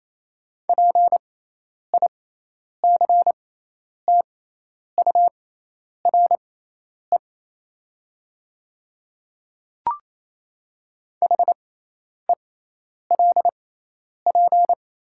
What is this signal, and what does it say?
Morse code, heard as a keyed tone, about 27 wpm, spaced as for 10 wpm, 700 Hz. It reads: PICTURE HELP
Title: 6x Character Spacing - Top 300 Words - 28wpm